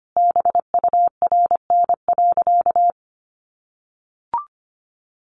Morse code, heard as a keyed tone, about 25 wpm, 700 Hz. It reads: BURN.
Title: Sentences from Top 1300 Words 25wpm